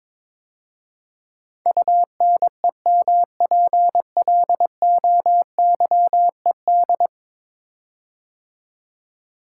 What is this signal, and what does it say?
Morse code, heard as a keyed tone, about 22 wpm, 700 Hz. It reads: UNEMPLOYED
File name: practice / 22wpm / 10 Letter Words - First 3 Letters Given - 22wpm